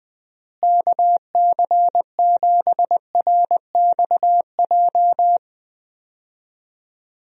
Morse code, heard as a keyed tone, about 21 wpm, 700 Hz. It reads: KC7RXJ
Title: USA Call Signs - 6 Characters - Repeated 3x - 20wpm